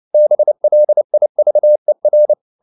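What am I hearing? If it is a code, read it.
Morse code, heard as a keyed tone, about 29 wpm, 600 Hz. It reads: BLIVER
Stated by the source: Fast (QRQ) Morse at 29 wpm, from CW med Gnister.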